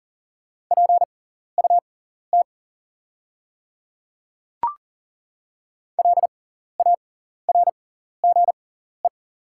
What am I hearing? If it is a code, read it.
Morse code, heard as a keyed tone, about 40 wpm, spaced as for 14 wpm, 700 Hz. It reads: PUT LARGE
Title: 6x Character Spacing - Top 200 Words - 40wpm